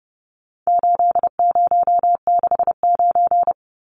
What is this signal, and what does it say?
Morse code, heard as a keyed tone, about 30 wpm, 700 Hz. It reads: 8069
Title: Numbers - 4-Digit 30wpm